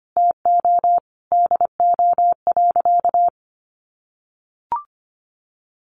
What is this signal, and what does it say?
Morse code, heard as a keyed tone, about 25 wpm, 700 Hz. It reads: TO DO.
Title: Sentences from Top 1300 Words 25wpm